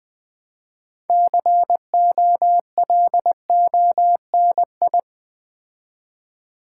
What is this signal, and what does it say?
Morse code, heard as a keyed tone, about 20 wpm, 700 Hz. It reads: COLONI
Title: Build A Word - 3 Letters At A Time - 8-Letter Words - 20wpm